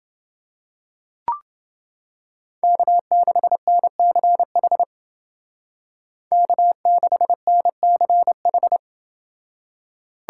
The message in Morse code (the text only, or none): K6NCH K6NCH